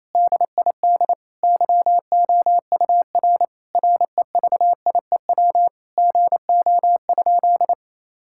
Morse code, heard as a keyed tone, about 28 wpm, 700 Hz. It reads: DID YOUR REVIEW GO?
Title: Sentences from Top 1000 Words - Review Part 6 - 28wpm